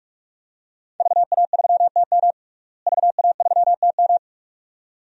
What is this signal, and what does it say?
Morse code, heard as a keyed tone, about 46 wpm, 700 Hz. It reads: VA3TM VA3TM
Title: DXCC-100 Call Signs - 5 Characters - Repeated 2x - 45wpm